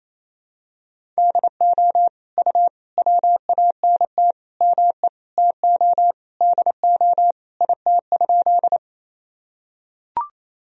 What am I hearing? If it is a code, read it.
Morse code, heard as a keyed tone, about 28 wpm, 700 Hz. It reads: DO U WANT ME TO DO IT?
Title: Sentences from Top 200 Words and Abbreviations 28wpm